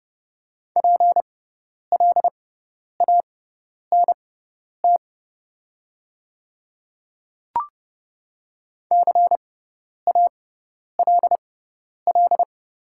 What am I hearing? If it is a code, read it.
Morse code, heard as a keyed tone, about 31 wpm, spaced as for 10 wpm, 700 Hz. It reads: PLANT CALL
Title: 6x Character Spacing - Top 300 Words - 30wpm